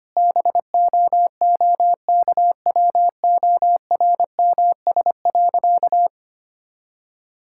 Morse code, heard as a keyed tone, about 25 wpm, 700 Hz. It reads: BOOKWORMS.
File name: practice / 25wpm / All-Age Appropriate Jokes - 25wpm